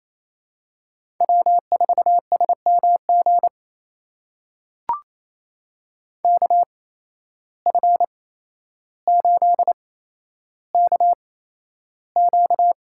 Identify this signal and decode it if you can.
Morse code, heard as a keyed tone, about 28 wpm, spaced as for 12 wpm, 700 Hz. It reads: W4SMG KF8KQ
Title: USA Call Signs - 5 Characters - 8x Character Spacing - 28wpm